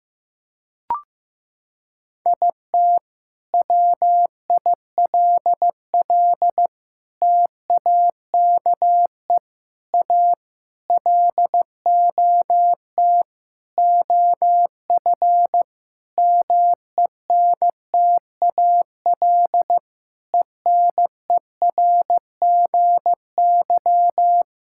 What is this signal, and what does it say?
Morse code, heard as a keyed tone, about 15 wpm, 700 Hz. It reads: IT WILL TAKE A LOT OF MENTAL ENERGY
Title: Sentences from Top 1000 Words - Review Part 6 - 15wpm